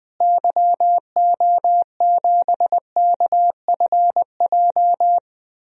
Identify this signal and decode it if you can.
Morse code, heard as a keyed tone, about 20 wpm, 700 Hz. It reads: YO7KFJ
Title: DXCC-100 Call Signs - 6 Characters - Repeated 2x - 20wpm